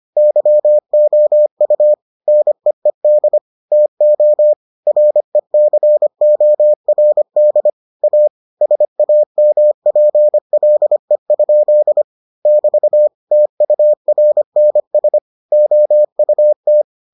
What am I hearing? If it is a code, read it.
Morse code, heard as a keyed tone, about 25 wpm, 600 Hz. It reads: YOU NEED TO RECORD A SAMPLE? <BT> TURNS OUT